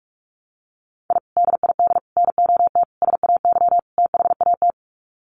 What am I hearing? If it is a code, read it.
Morse code, heard as a keyed tone, about 44 wpm, 700 Hz. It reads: I DID NOT SAY THAT
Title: Sets of 5 Words Encore2 45wpm